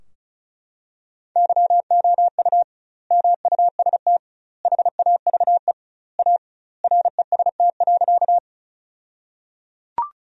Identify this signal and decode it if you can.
Morse code, heard as a keyed tone, about 34 wpm, 700 Hz. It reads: YOU MUST HAVE A REST.
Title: 2x Word-Spacing - Sentences from Top 500 Words - 35wpm